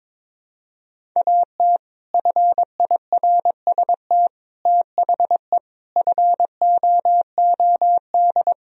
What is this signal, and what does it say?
Morse code, heard as a keyed tone, about 22 wpm, 700 Hz. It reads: AT FIRST THE FOOD
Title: Sentences from Top 1000 Words - Review Part 1 - 22wpm